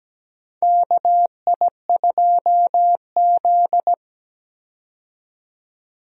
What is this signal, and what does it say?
Morse code, heard as a keyed tone, about 17 wpm, 700 Hz. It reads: KI2Z
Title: USA Call Signs - 4 Characters - 17wpm